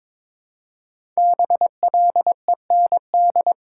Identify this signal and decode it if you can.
Morse code, heard as a keyed tone, about 22 wpm, 690 Hz. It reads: BLEND
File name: 5 Letter Words - First 3 Letters Given - 22wpm